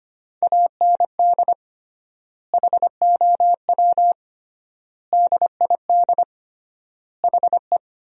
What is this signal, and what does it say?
Morse code, heard as a keyed tone, about 25 wpm, 700 Hz. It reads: AND HOW DID HE
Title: 3x Word-Spacing - Sentences from Top 400 Words - 25wpm